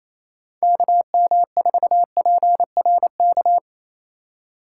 Morse code, heard as a keyed tone, about 35 wpm, 700 Hz. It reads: KM4PRK
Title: Speed Racing - USA Call Signs 28wpm